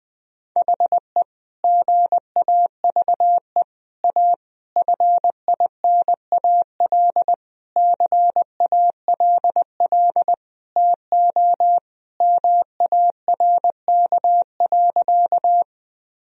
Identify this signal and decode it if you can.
Morse code, heard as a keyed tone, about 20 wpm, 700 Hz. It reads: HE GAVE A FINAL CALL TO MARK.
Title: Sentences from Top 1000 Words - Review Part 2 - 20wpm